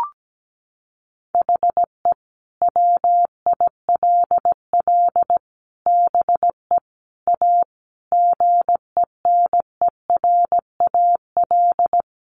Morse code, heard as a keyed tone, about 17 wpm, 700 Hz. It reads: HE WILL BE A GENERAL
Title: Sentences from Top 1200 Words 17wpm